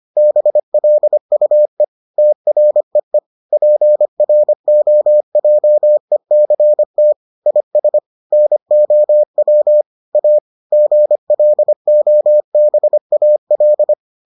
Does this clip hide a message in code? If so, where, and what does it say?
Yes: Morse code from 0.2 to 13.9 s, heard as a keyed tone, about 25 wpm, 600 Hz. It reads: BLUE TREE PROJECT IS NOW A GLOBAL